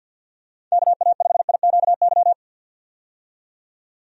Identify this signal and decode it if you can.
Morse code, heard as a keyed tone, about 52 wpm, 700 Hz. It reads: KA5IQY
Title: USA Call Signs - 6 Characters - 50wpm